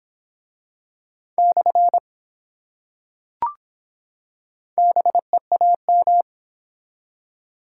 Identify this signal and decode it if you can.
Morse code, heard as a keyed tone, about 23 wpm, 700 Hz. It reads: / BEAM